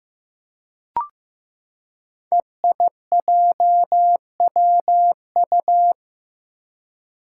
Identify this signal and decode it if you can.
Morse code, heard as a keyed tone, about 15 wpm, 700 Hz. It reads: EIJWU
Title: Five Challenging Letter-Number Combinations 15wpm